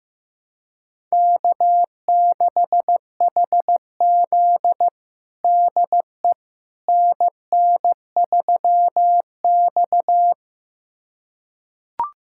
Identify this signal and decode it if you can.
Morse code, heard as a keyed tone, about 15 wpm, 700 Hz. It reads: K6HZ DE NN3X
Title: USA Callsign DE Callsign - 4 Characters - 15wpm